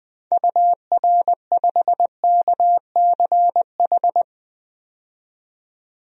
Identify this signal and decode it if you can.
Morse code, heard as a keyed tone, about 20 wpm, 700 Hz. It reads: UR5KCH
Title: DXCC-100 Call Signs - 6 Characters - 20wpm